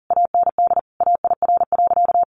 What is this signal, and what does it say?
Morse code, heard as a keyed tone, about 40 wpm, 700 Hz. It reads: AND AIR.
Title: Sentences from Top 400 Words 40wpm